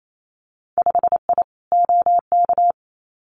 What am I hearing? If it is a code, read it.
Morse code, heard as a keyed tone, about 28 wpm, 700 Hz. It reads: 5I OK
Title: Winter Field Day Exchanges Encore 28wpm